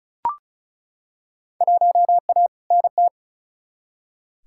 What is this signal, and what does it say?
Morse code, heard as a keyed tone, about 36 wpm, 700 Hz. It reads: 1A NT
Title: Field Day Exchanges Encore - 35wpm